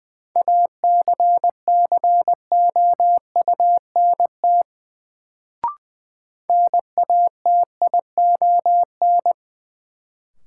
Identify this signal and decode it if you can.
Morse code, heard as a keyed tone, about 20 wpm, 700 Hz. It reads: ACCOUNT NATION
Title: Sentences from Top 900 Words - New Word Drill - 20wpm